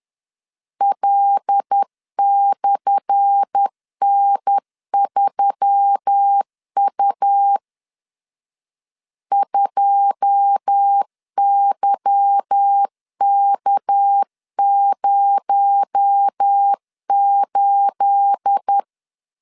Morse code, heard as a keyed tone, about 10 wpm, 800 Hz. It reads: L/N3U 2YK08
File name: LEZIONI / 10 SIGNS / 10A SLASH 10wpm 2808